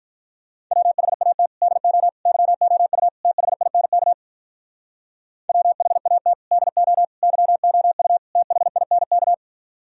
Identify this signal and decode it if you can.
Morse code, heard as a keyed tone, about 47 wpm, 700 Hz. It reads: WHAT DO YOU THINK WHAT DO YOU THINK